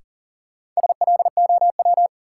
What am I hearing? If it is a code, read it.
Morse code, heard as a keyed tone, about 40 wpm, 700 Hz. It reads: SLOW
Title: Sentences from Top 400 Words - New Word Drill - 40wpm